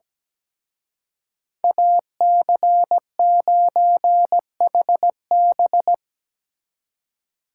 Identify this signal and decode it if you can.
Morse code, heard as a keyed tone, about 19 wpm, 700 Hz. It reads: AC9HB
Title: Speed Racing - USA 2x2 Call Signs 17wpm